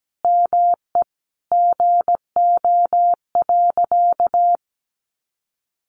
Morse code, heard as a keyed tone, about 17 wpm, 700 Hz. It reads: ME GO.
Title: Sentences from Top 1500 Words 17wpm